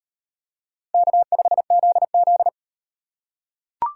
Morse code, heard as a keyed tone, about 37 wpm, 700 Hz. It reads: K5ZZ